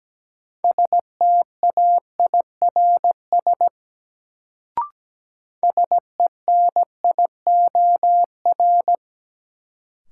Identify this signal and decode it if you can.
Morse code, heard as a keyed tone, about 17 wpm, 700 Hz. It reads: STAIRS SENIOR
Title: Sets of 5 Words Encore - New Word Drill - 17wpm